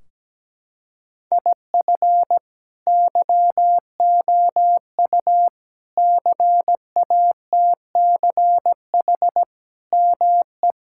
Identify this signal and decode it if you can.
Morse code, heard as a keyed tone, about 17 wpm, 700 Hz. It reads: IF YOU CATCH ME
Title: Sentences from Top 1600 Words 17wpm